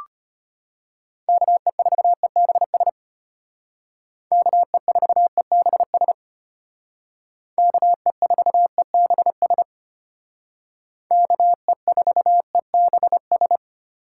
Morse code, heard as a keyed tone, about 33 wpm, 700 Hz. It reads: KE4EBS KE4EBS KE4EBS KE4EBS